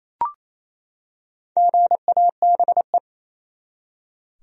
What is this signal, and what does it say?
Morse code, heard as a keyed tone, about 27 wpm, 700 Hz. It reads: GABE
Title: USA Common Names - Max 4 Letters - 28wpm